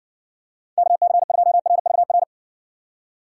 Morse code, heard as a keyed tone, about 60 wpm, 700 Hz. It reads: KC2RVR